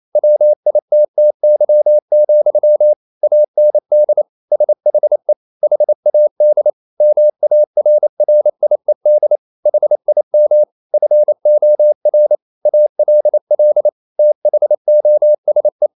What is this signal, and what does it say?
Morse code, heard as a keyed tone, about 28 wpm, 600 Hz. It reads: WITTY, AND SHE HAD MARRIED HIM FOR ALL THOSE
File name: Never 0027wpm